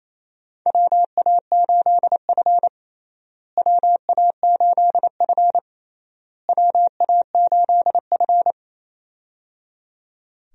Morse code, heard as a keyed tone, about 28 wpm, 700 Hz. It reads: WA8F WA8F WA8F
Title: USA Call Signs - 4 Characters - Repeated 3x - 28wpm